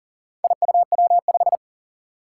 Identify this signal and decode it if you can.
Morse code, heard as a keyed tone, about 40 wpm, 700 Hz. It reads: IUW5
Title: Four Challenging Letter-Number Combinations 40wpm